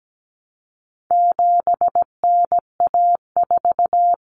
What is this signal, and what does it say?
Morse code, heard as a keyed tone, about 17 wpm, 700 Hz. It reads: 7NA4
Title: Common Cut Numbers - 4-Digit 17wpm